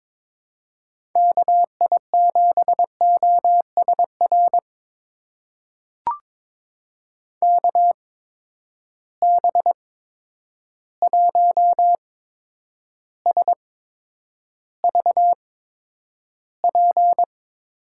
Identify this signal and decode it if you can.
Morse code, heard as a keyed tone, about 22 wpm, spaced as for 9 wpm, 700 Hz. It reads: KI7OSR KB1SVP
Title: USA Call Signs - 6 Characters - 8x Character Spacing - 22wpm